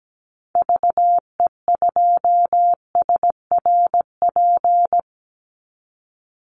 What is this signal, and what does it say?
Morse code, heard as a keyed tone, about 17 wpm, 700 Hz. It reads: VE2SRP
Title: Call Signs Repeated 3x - 17wpm